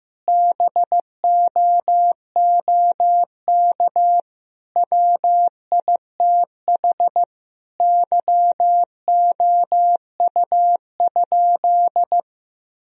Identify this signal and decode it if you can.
Morse code, heard as a keyed tone, about 15 wpm, 700 Hz. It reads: BOOK WITH YOU?
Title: Sentences from Top 1000 Words - Review Part 3 - 15wpm